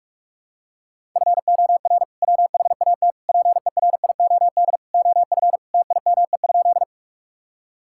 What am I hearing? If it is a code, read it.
Morse code, heard as a keyed tone, about 44 wpm, 700 Hz. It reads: FOR WHAT PERIOD OF TIME?
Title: Sentences from Top 1000 Words - Review Part 5 - 45wpm